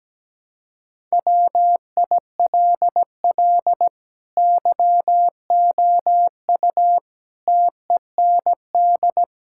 Morse code, heard as a keyed tone, about 17 wpm, 700 Hz. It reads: WILL YOU TEND
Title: Sentences from Top 1000 Words - Review Part 5 - 17wpm